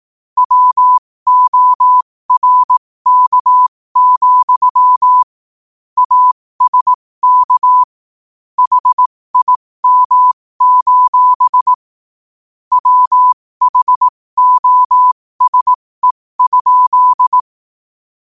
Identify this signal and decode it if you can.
Morse code, heard as a keyed tone, about 18 wpm, spaced as for 15 wpm, 1000 Hz. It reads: WORK, ASK HIM: WHOSE?
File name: qotdfiles / 15wpm